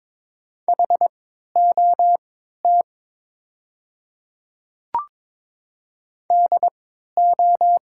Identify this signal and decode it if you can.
Morse code, heard as a keyed tone, about 22 wpm, spaced as for 13 wpm, 700 Hz. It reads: HOT DO
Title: 3x Character Spacing - Top 100 Words - 22wpm